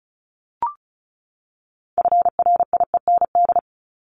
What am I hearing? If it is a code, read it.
Morse code, heard as a keyed tone, about 34 wpm, 700 Hz. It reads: FRIEND
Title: Top 500 Words 35wpm